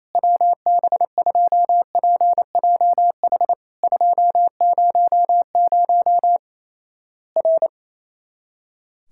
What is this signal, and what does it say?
Morse code, heard as a keyed tone, about 28 wpm, 700 Hz. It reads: WB2PJH 200 R
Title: Worked All Europe DX Contest - QTC Exchange - 28wpm